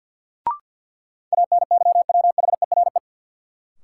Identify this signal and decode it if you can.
Morse code, heard as a keyed tone, about 53 wpm, 700 Hz. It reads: ANYWHERE
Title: Sentences from Top 1400 Words - New Word Drill Alt - 50wpm